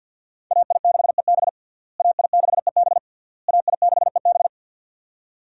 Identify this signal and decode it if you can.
Morse code, heard as a keyed tone, about 52 wpm, 700 Hz. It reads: AI6EB AI6EB AI6EB
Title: USA Call Signs - 5 Characters - Repeated 3x - 50wpm